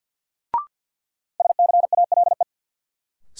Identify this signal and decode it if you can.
Morse code, heard as a keyed tone, about 53 wpm, 700 Hz. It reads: SCALE
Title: Sentences from Top 1000 Words - Review Part 4 - New Word Drill - 50wpm